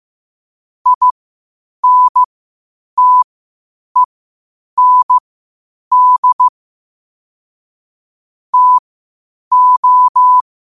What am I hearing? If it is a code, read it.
Morse code, heard as a keyed tone, about 15 wpm, spaced as for 8 wpm, 1000 Hz. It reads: INTEND TO
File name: qotdfiles / 8wpm